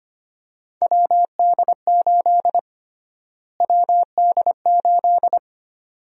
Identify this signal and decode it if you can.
Morse code, heard as a keyed tone, about 25 wpm, 700 Hz. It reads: WD8 WD8
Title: USA Call Sign Prefixes - Repeated 2x - 25wpm